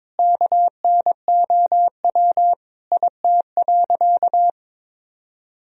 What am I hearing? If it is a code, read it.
Morse code, heard as a keyed tone, about 22 wpm, 700 Hz. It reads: KNOW IT.
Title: Sentences from Top 1000 Words - Review Part 2 - 22wpm